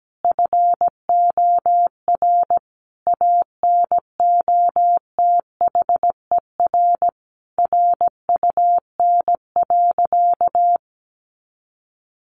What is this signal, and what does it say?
Morse code, heard as a keyed tone, about 17 wpm, 700 Hz. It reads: FOR ANOTHER RUN.
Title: Sentences from Top 1500 Words 17wpm